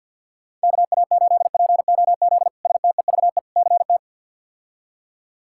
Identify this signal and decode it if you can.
Morse code, heard as a keyed tone, about 53 wpm, 700 Hz. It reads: KA8POZ STEVE CT
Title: K1USN SST QSO Practice - 50wpm